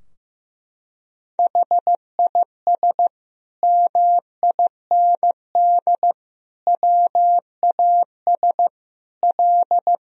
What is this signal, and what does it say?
Morse code, heard as a keyed tone, about 16 wpm, 700 Hz. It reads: HIS MIND WAS L
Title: Sentences from Top 1000 Words - Review Part 2 - 15wpm